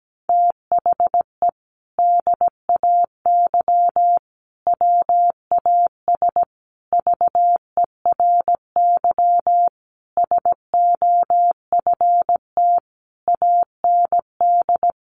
Morse code, heard as a keyed tone, about 17 wpm, 700 Hz. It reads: THE DAY WAS VERY SOFT AND
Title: Sentences from Top 1500 Words 17wpm